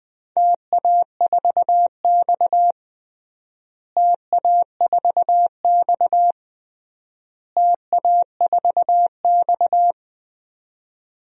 Morse code, heard as a keyed tone, about 20 wpm, 700 Hz. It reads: TA4X TA4X TA4X
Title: DXCC-100 Call Signs - 4 Characters - Repeated 3x - 20wpm